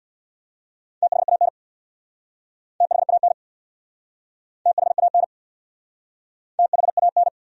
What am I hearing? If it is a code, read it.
Morse code, heard as a keyed tone, about 66 wpm, 700 Hz. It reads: THAN THAN THAN THAN